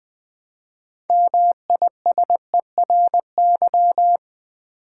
Morse code, heard as a keyed tone, about 20 wpm, 700 Hz. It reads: MISERY
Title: Build A Word - 3 Letters At A Time - 6-Letter Words - 20wpm